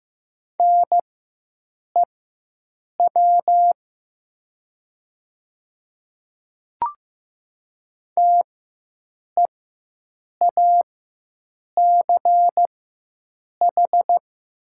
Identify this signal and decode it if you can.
Morse code, heard as a keyed tone, about 15 wpm, spaced as for 7 wpm, 700 Hz. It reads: NEW TEACH
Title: 4x Character Spacing - Top 500 Words - 15wpm